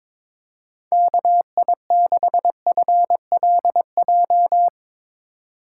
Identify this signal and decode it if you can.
Morse code, heard as a keyed tone, about 22 wpm, 700 Hz. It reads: KI6FLJ
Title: USA Call Signs - 6 Characters - 4x Character Spacing - 22wpm